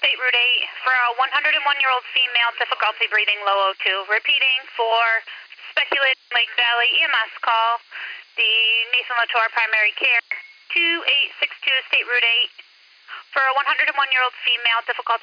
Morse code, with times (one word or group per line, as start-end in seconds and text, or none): none